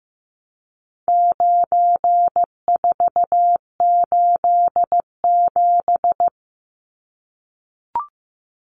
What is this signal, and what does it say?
Morse code, heard as a keyed tone, about 15 wpm, 700 Hz. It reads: 9487
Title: Numbers - 4-Digit 15wpm